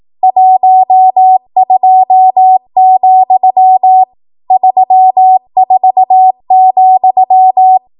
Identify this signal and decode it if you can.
Morse code, heard as a keyed tone, about 18 wpm, 750 Hz. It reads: 12, 34,
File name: Archive / 18 WPM / 18WPM